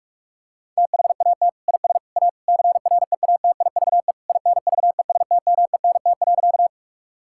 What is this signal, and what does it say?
Morse code, heard as a keyed tone, about 47 wpm, 700 Hz. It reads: THAT IS A CREATIVE INVESTMENT.
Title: Sentences from Top 1000 Words - Review Part 5 - 45wpm